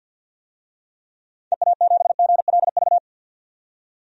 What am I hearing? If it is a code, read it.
Morse code, heard as a keyed tone, about 47 wpm, 700 Hz. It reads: EA7GLV
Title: DXCC-100 Call Signs - 6 Characters - 50wpm